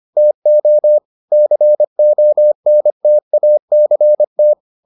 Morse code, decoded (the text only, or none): TO CONTACT